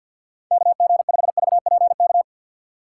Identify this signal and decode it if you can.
Morse code, heard as a keyed tone, about 52 wpm, 700 Hz. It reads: KG5VPK
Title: USA Call Signs - 6 Characters - 4x Character Spacing - 50wpm